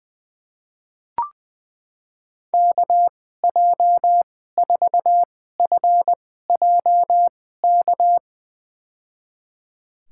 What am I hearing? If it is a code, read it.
Morse code, heard as a keyed tone, about 20 wpm, spaced as for 14 wpm, 700 Hz. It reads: KJ4FJK
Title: USA Call Signs - 6 Characters - 2x Character Spacing - 20wpm